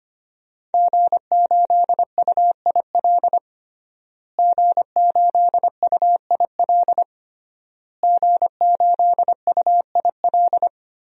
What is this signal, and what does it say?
Morse code, heard as a keyed tone, about 25 wpm, 700 Hz. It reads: G8UIL G8UIL G8UIL
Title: DXCC-100 Call Signs - 5 Characters - Repeated 3x - 25wpm